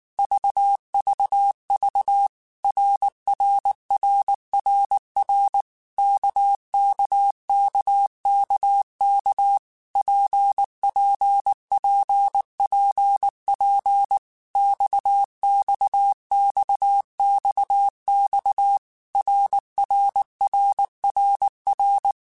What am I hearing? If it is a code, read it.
Morse code, heard as a keyed tone, about 19 wpm, 800 Hz. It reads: VVV RRRRR KKKKK PPPPP XXXXX RRRRR
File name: Basic Letter Groups By7Groups / RKPX / RKPX@18